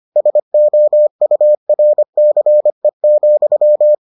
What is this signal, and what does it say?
Morse code, heard as a keyed tone, about 25 wpm, 600 Hz. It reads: SOURCE,